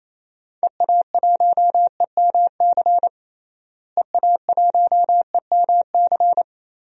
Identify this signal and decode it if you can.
Morse code, heard as a keyed tone, about 28 wpm, 700 Hz. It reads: EA1EMC EA1EMC
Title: DXCC-100 Call Signs - 6 Characters - Repeated 3x - 28wpm